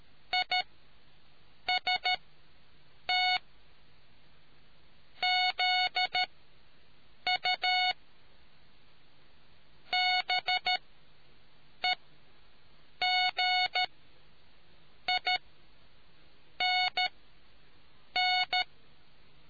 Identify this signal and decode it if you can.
Morse code, heard as a keyed tone, about 13 wpm, spaced as for 6 wpm, 730 Hz. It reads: IST ZU BEGINN